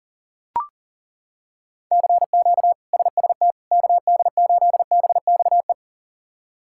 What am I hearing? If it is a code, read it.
Morse code, heard as a keyed tone, about 40 wpm, 690 Hz. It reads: CQ SST KD8BXE
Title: K1USN SST QSO Practice - 40wpm